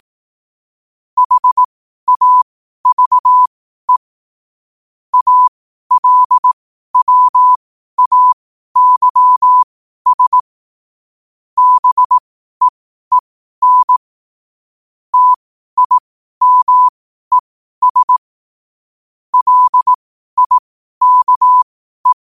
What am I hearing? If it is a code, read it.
Morse code, heard as a keyed tone, about 18 wpm, spaced as for 12 wpm, 1000 Hz. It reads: HAVE ALWAYS BEEN TIMES LIKE
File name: qotdfiles / 12wpm